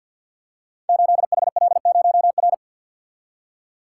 Morse code, heard as a keyed tone, about 52 wpm, 700 Hz. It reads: 8HL0R
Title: Code Groups - 5 Letter:Numbers - 50wpm